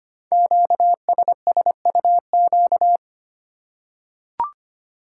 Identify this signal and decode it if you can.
Morse code, heard as a keyed tone, about 25 wpm, 700 Hz. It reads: QSSUQ
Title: Code Groups - 5 Letters - 25wpm